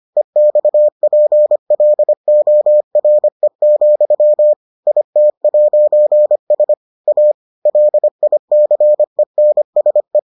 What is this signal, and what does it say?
Morse code, heard as a keyed tone, about 25 wpm, 600 Hz. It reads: EXPLORE, IT'S A LICENSE